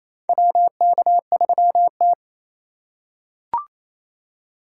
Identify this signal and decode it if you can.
Morse code, heard as a keyed tone, about 28 wpm, 700 Hz. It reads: WK3T
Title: USA Call Signs - 4 Characters - 3x Character Spacing - 28wpm